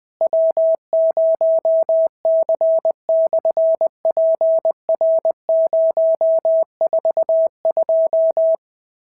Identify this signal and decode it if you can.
Morse code, heard as a keyed tone, about 20 wpm, 650 Hz. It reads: W0C/PR042
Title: SOTA QSO Practice 20wpm